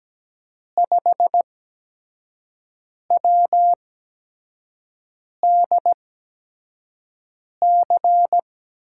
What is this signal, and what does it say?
Morse code, heard as a keyed tone, about 17 wpm, spaced as for 5 wpm, 700 Hz. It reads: 5WDC
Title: USA Call Signs - 8x Character Spacing - 17wpm